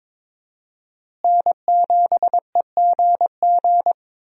Morse code, heard as a keyed tone, about 22 wpm, 700 Hz. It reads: N7EGG
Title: USA Call Signs - 5 Characters - Repeated 3x - 22wpm